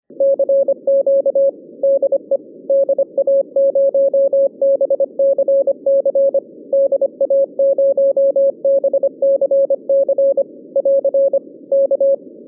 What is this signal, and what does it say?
Morse code, heard as a keyed tone, about 25 wpm, 550 Hz. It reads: CQ DE DA0BCC DA0BCC <AR> K
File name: 2011 bcc-challenge samstag tape